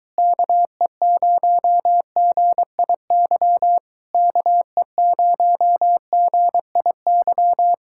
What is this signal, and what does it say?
Morse code, heard as a keyed tone, about 23 wpm, 700 Hz. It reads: KE0GIY KE0GIY